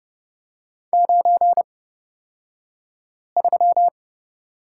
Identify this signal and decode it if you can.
Morse code, heard as a keyed tone, about 30 wpm, spaced as for 5 wpm, 700 Hz. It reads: 93
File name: Lesson 36d - BT - Two Characters 30wpm